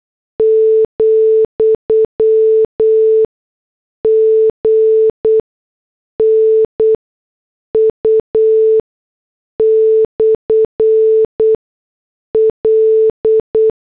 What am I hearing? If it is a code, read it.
Morse code, heard as a keyed tone, about 8 wpm, spaced as for 6 wpm, 440 Hz. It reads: , GNU/L